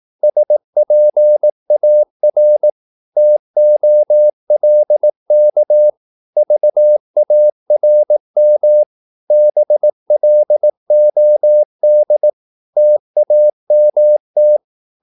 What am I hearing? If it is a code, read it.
Morse code, heard as a keyed tone, about 18 wpm, 600 Hz. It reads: SPAR TOLK VARM BLOD TAMT
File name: Korte ord DK 18wpm